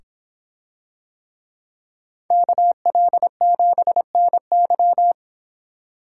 Morse code, heard as a keyed tone, about 24 wpm, 700 Hz. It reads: KL7NY